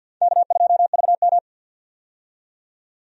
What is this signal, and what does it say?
Morse code, heard as a keyed tone, about 52 wpm, 700 Hz. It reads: K2VM
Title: USA Call Signs - 4 Characters - 50wpm